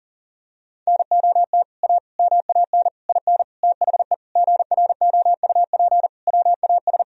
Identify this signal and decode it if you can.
Morse code, heard as a keyed tone, about 40 wpm, 700 Hz. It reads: NOT A MAN IN THE GROUP WAS
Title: Sentences from Top 1000 Words - Review Part 1 - 40wpm